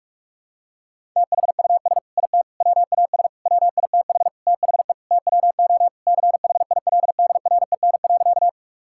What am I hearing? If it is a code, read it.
Morse code, heard as a keyed tone, about 46 wpm, 700 Hz. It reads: THUS IT WAS WITH THE TWO CHILDREN.
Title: Sentences from Top 1000 Words - Review Part 5 - 45wpm